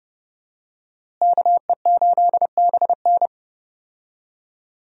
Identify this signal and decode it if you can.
Morse code, heard as a keyed tone, about 30 wpm, 700 Hz. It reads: KE8BN
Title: USA Call Signs - 5 Characters - Repeated 3x - 30wpm